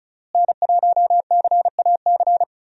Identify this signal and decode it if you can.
Morse code, heard as a keyed tone, about 36 wpm, 700 Hz. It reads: N1CAC
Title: USA Call Signs - 5 Characters - Repeated 2x - 35wpm